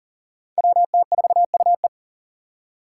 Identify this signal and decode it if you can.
Morse code, heard as a keyed tone, about 40 wpm, 700 Hz. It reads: WT4UE
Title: Five Challenging Letter-Number Combinations 40wpm